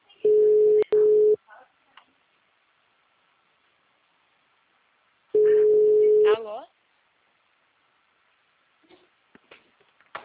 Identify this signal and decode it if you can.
Ringback tone (Europe, 425 Hz), with speech.
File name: out- -2012- - - .35226